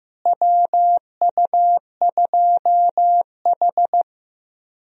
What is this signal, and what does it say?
Morse code, heard as a keyed tone, about 15 wpm, 700 Hz. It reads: WU2H
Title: USA Call Signs - 4 Characters - Repeated 3x - 15wpm